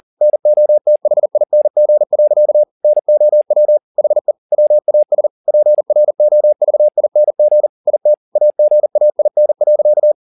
Morse code, heard as a keyed tone, about 40 wpm, 600 Hz. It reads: NOTHING. NOW HE WAS PROVING IT AGAIN.